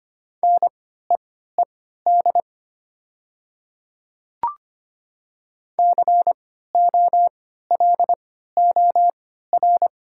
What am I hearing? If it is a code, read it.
Morse code, heard as a keyed tone, about 25 wpm, spaced as for 14 wpm, 700 Hz. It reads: NEED COLOR
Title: 3x Character Spacing - Top 300 Words - 25wpm